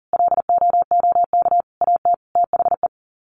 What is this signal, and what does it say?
Morse code, heard as a keyed tone, about 40 wpm, 700 Hz. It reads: LOOK AT THE
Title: Sets of 5 Words 40wpm